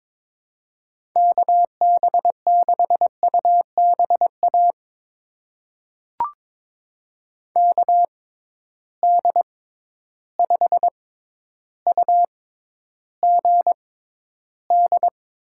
Morse code, heard as a keyed tone, about 23 wpm, spaced as for 11 wpm, 700 Hz. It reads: KB6UBA KD5UGD
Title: USA Call Signs - 6x Character Spacing - 22wpm